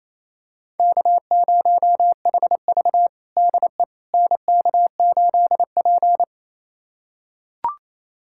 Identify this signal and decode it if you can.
Morse code, heard as a keyed tone, about 27 wpm, 700 Hz. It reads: K0HV DE NK8P
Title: USA Callsign DE Callsign - 4 Characters - 28wpm